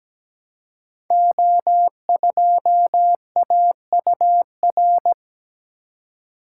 Morse code, heard as a keyed tone, about 17 wpm, 700 Hz. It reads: O2AUR
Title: Code Groups - 5 Letter:Numbers - 17wpm